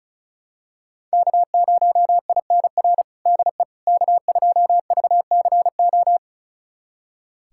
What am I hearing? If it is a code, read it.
Morse code, heard as a keyed tone, about 36 wpm, 700 Hz. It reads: K0INR DE K2VCO
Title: USA Callsign DE Callsign - 5 Characters - 35wpm